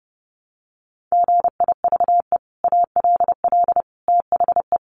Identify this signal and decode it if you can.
Morse code, heard as a keyed tone, about 28 wpm, 700 Hz. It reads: GIVE ALL THE
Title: Sentences from Top 600 Words 30wpm